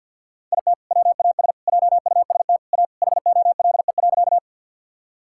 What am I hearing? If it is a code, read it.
Morse code, heard as a keyed tone, about 50 wpm, 700 Hz. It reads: IT WAS JUST A HOLE.
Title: Sentences from Top 1600 Words 50wpm